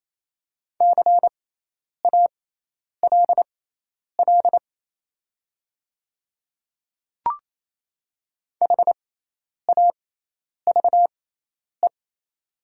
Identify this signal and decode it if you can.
Morse code, heard as a keyed tone, about 28 wpm, spaced as for 10 wpm, 690 Hz. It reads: CALL HAVE
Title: 6x Character Spacing - Top 500 Words - 28wpm